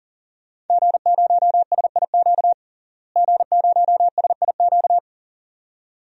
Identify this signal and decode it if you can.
Morse code, heard as a keyed tone, about 40 wpm, 700 Hz. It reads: G0SIQ G0SIQ
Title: DXCC-100 Call Signs - 5 Characters - Repeated 2x - 40wpm